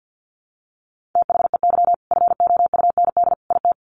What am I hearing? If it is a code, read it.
Morse code, heard as a keyed tone, about 50 wpm, 700 Hz. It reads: THEY FOUND IT
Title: Sentences from Top 1400 Words 50wpm